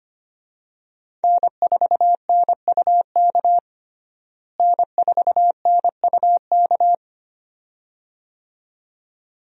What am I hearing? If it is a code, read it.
Morse code, heard as a keyed tone, about 25 wpm, 700 Hz. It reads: N4NUK N4NUK
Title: USA Call Signs - 5 Characters - Repeated 2x - 25wpm